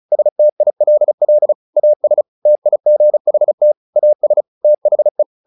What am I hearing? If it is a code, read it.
Morse code, heard as a keyed tone, about 34 wpm, 600 Hz. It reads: STILL AS TIGHT AS THE
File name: the old man and the sea / the old man and the sea 1 35wpm